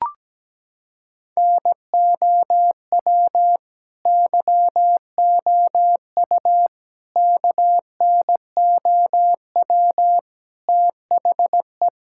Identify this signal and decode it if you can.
Morse code, heard as a keyed tone, about 17 wpm, 700 Hz. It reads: NOW YOU KNOW THE
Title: Sentences from Top 1000 Words - Review Part 6 - 17wpm